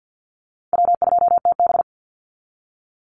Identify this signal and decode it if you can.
Morse code, heard as a keyed tone, about 50 wpm, 700 Hz. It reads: W2TB
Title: Easy Call Signs Repeated 2x - 50wpm